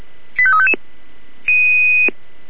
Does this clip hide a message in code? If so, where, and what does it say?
Yes: five-tone selective call (ZVEI) from 0.4 to 0.7 s, heard as quick runs of five tones, tones of about 70 ms. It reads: 85355